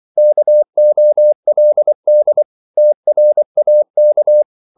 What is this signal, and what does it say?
Morse code, heard as a keyed tone, about 24 wpm, 600 Hz. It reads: KOLD TRAK
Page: Korte danske ord 24wpm